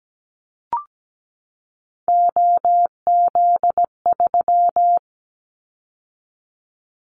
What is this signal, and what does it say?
Morse code, heard as a keyed tone, about 17 wpm, 700 Hz. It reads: OZ3
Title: Callsign Prefixes for Top 100 DXCC - 17wpm